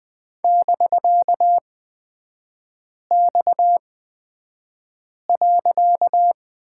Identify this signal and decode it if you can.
Morse code, heard as a keyed tone, about 20 wpm, spaced as for 5 wpm, 700 Hz. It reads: <BK>X.
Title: 255 Lesson 41e - Period - Three Characters 20wpm